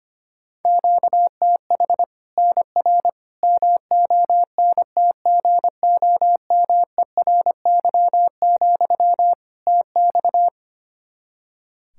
Morse code, heard as a keyed tone, about 25 wpm, 700 Hz. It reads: QTH NR MONTGOMERY, TX
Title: Practice QSO - Exchange 1 - Beginning - 25wpm